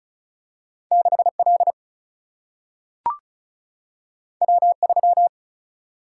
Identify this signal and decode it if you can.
Morse code, heard as a keyed tone, about 36 wpm, 700 Hz. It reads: 6 L W 3
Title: Single-Letter Single-Number 35wpm